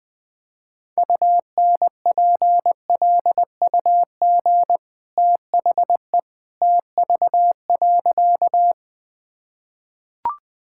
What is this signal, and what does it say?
Morse code, heard as a keyed tone, about 19 wpm, 700 Hz. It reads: UNPLUG THE TV.
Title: Easy Sentences with Hints - 20wpm